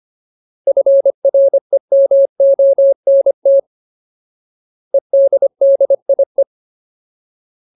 Morse code, heard as a keyed tone, about 25 wpm, 550 Hz. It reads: FREMONT EDDIE
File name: Ultimate Word List / 25WPM Ultimate